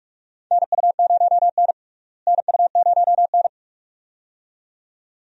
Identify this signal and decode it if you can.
Morse code, heard as a keyed tone, about 45 wpm, 700 Hz. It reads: NU0N NU0N
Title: USA Call Signs - 4 Characters - Repeated 2x - 45wpm